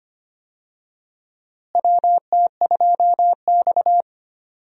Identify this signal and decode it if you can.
Morse code, heard as a keyed tone, about 25 wpm, 700 Hz. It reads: WT2X
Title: USA Call Signs - 4 Characters - Repeated 2x - 25wpm